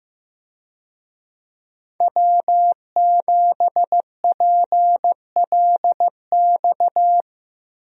Morse code, heard as a keyed tone, about 16 wpm, 700 Hz. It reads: W7PLX
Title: USA Call Signs - 5 Characters - Repeated 3x - 15wpm